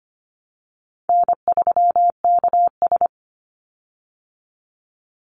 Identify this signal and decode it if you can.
Morse code, heard as a keyed tone, about 25 wpm, 700 Hz. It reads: N3KS
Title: Easy Call Signs Repeated 3x - 25wpm